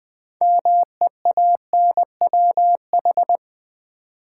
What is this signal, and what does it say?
Morse code, heard as a keyed tone, about 20 wpm, 700 Hz. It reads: MEANWH
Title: Build A Word - 3 Letters At A Time - 9-Letter Words - 20wpm